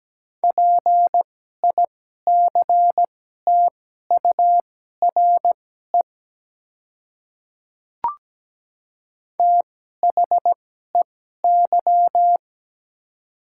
Morse code, heard as a keyed tone, about 17 wpm, spaced as for 12 wpm, 700 Hz. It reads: PICTURE THEY
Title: 2x Character Spacing - Top 200 Words - 17wpm